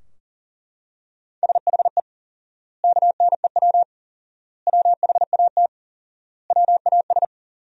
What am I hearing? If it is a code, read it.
Morse code, heard as a keyed tone, about 40 wpm, 700 Hz. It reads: SHE KNEW WHAT WAS
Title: 4x Word-Spacing - Sentences from Top 400 Words - 40wpm